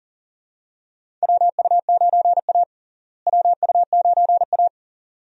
Morse code, heard as a keyed tone, about 38 wpm, 700 Hz. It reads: WU9A WU9A
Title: USA Call Signs - 4 Characters - Repeated 2x - 40wpm